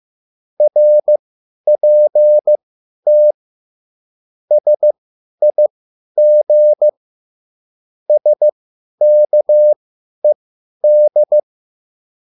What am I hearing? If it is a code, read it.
Morse code, heard as a keyed tone, about 15 wpm, spaced as for 10 wpm, 600 Hz. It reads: RPT SIG SKED
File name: QSL / CW lesson